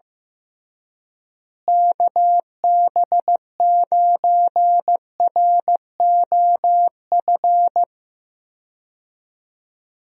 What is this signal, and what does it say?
Morse code, heard as a keyed tone, about 15 wpm, 700 Hz. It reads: KB9ROF
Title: USA Call Signs - 6 Characters - Repeated 3x - 15wpm